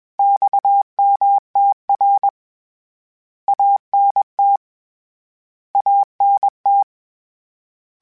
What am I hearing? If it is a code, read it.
Morse code, heard as a keyed tone, about 20 wpm, 800 Hz. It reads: XMTR ANT ANT
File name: Abreviations / Abreviations mp3s / Abreviations@20